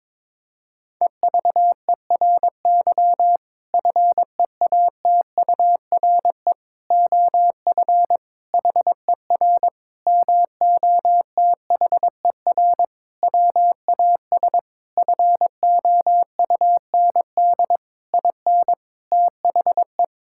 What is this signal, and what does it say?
Morse code, heard as a keyed tone, about 22 wpm, 700 Hz. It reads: EVERY FEATURE OF HER MOTHER WAS FOUND IN THE
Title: Sentences from Top 1000 Words - Review Part 5 - 22wpm